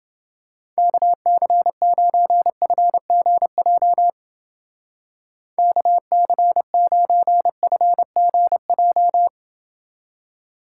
Morse code, heard as a keyed tone, about 29 wpm, 700 Hz. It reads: KC9FGJ KC9FGJ